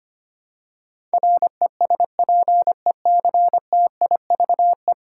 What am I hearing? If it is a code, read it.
Morse code, heard as a keyed tone, about 25 wpm, 700 Hz. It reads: RESPECTIVE
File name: 10 Letter Words - First 3 Letters Given - 25wpm